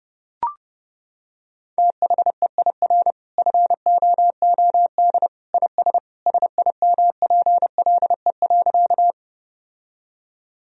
Morse code, heard as a keyed tone, about 32 wpm, 700 Hz. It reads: THEIR FOOD IS SIMPLE.
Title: Sentences from Top 1000 Words - Review Part 1 - 30wpm